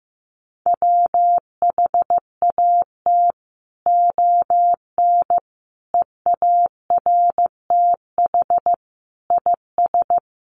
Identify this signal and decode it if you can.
Morse code, heard as a keyed tone, about 15 wpm, 700 Hz. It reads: WHAT ON EARTH IS
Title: Sentences from Top 1500 Words 15wpm